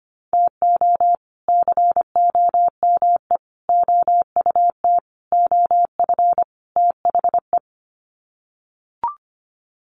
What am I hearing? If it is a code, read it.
Morse code, heard as a keyed tone, about 25 wpm, 700 Hz. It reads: TO COME OUT OF THE
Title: Sets of 5 Words Encore2 25wpm